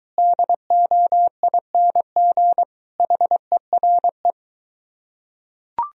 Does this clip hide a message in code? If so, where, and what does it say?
Yes: Morse code from 0.2 to 4.3 s, heard as a keyed tone, about 23 wpm, 700 Hz. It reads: DOING HERE